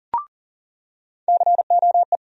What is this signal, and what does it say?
Morse code, heard as a keyed tone, about 40 wpm, 700 Hz. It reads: COE
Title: USA Call Sign Postfixes - 3 Letters 40wpm